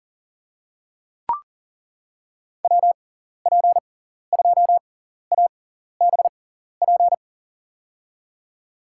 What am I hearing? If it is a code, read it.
Morse code, heard as a keyed tone, about 40 wpm, spaced as for 14 wpm, 700 Hz. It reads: WP2ABP
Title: USA Call Signs - 6 Characters - 6x Character Spacing - 40wpm